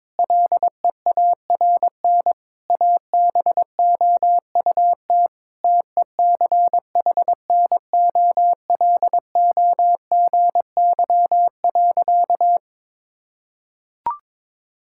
Morse code, heard as a keyed tone, about 22 wpm, 700 Hz. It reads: LEARN ABOUT TECHNOLOGY.
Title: Sentences from Top 1000 Words - Review Part 5 - 22wpm